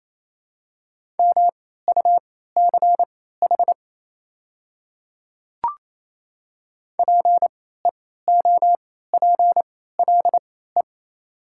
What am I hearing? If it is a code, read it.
Morse code, heard as a keyed tone, about 28 wpm, spaced as for 16 wpm, 700 Hz. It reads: MUCH PEOPLE
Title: 3x Character Spacing - Top 200 Words - 28wpm